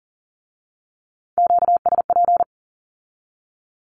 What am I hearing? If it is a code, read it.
Morse code, heard as a keyed tone, about 41 wpm, 700 Hz. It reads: QSP
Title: Q-Codes with Spoken Meaning - 40wpm